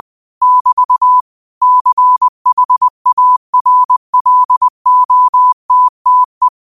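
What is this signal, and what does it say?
Morse code, heard as a keyed tone, about 20 wpm, 1000 Hz. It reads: <BT> CHARLOTTE
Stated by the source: Morse quote practice at 20 wpm.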